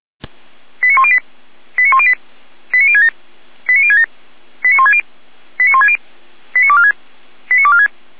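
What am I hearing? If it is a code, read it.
Five-tone selective call (ZVEI), heard as quick runs of five tones, tones of about 70 ms. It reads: 89108 89108 89067 89067 89160 89160 89246 89246
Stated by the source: ZVEI five-tone selective call received by SDR.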